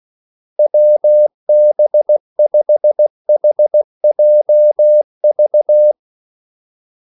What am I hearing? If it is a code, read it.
Morse code, heard as a keyed tone, about 16 wpm, 600 Hz. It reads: WB5HJV
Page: Kaldesignaler 16 wpm